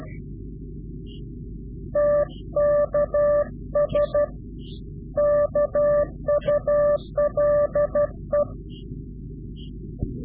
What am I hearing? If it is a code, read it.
Morse code, heard as a keyed tone, about 12 wpm, 590 Hz. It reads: TKS KULE
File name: CW-QSO / Directional Call / Directional QSO